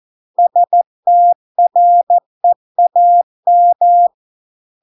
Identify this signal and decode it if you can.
Morse code, heard as a keyed tone, about 14 wpm, 700 Hz. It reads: STREAM